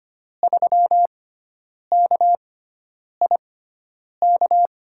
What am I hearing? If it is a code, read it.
Morse code, heard as a keyed tone, about 25 wpm, spaced as for 9 wpm, 700 Hz. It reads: 3KIK
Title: USA Call Signs - 6 Characters - 6x Character Spacing - 25wpm